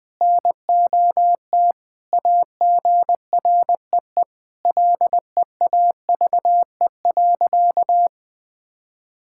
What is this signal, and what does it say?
Morse code, heard as a keyed tone, about 20 wpm, 700 Hz. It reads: NOT AGREE LEAVE.
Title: Sentences with 4-Word Phrases - 20wpm